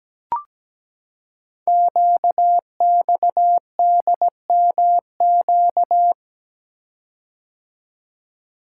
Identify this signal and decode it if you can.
Morse code, heard as a keyed tone, about 17 wpm, 700 Hz. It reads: QXDMQ
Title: Code Groups - 5 Letters - 17wpm